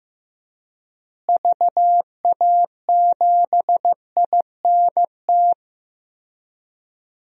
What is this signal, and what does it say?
Morse code, heard as a keyed tone, about 15 wpm, 700 Hz. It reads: VA7INT
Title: DXCC-100 Call Signs - 6 Characters - Repeated 3x - 15wpm